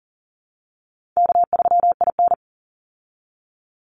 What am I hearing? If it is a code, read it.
Morse code, heard as a keyed tone, about 40 wpm, 700 Hz. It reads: K3IN
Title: Call Signs Repeated 3x - 40wpm